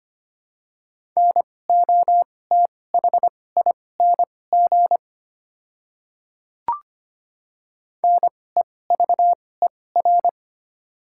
Morse code, heard as a keyed tone, about 25 wpm, spaced as for 18 wpm, 690 Hz. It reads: NOTHING NEVER
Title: 2x Character Spacing - Top 500 Words - 25wpm